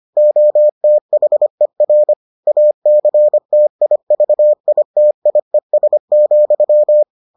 Morse code, heard as a keyed tone, about 25 wpm, 600 Hz. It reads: OTHER ACTIVITIES,